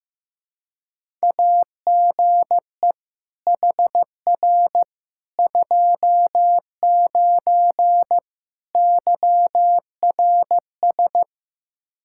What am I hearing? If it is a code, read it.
Morse code, heard as a keyed tone, about 15 wpm, 700 Hz. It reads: AGE HR 29 YRS
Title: Practice QSO - Exchange 3 - More Info - 15wpm